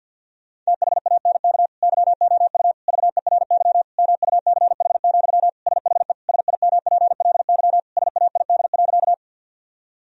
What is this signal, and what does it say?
Morse code, heard as a keyed tone, about 50 wpm, 700 Hz. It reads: THANK YOU VERY MUCH, SHE SIMPLY SAID.
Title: Sentences from Top 1000 Words - Review Part 4 - 50wpm